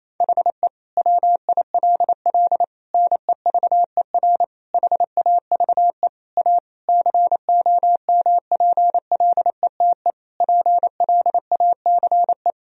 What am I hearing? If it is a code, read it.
Morse code, heard as a keyed tone, about 28 wpm, 700 Hz. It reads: HE WILL NEVER HAVE A COMPLETE PLACE
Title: Sentences from Top 1000 Words - Review Part 2 - 28wpm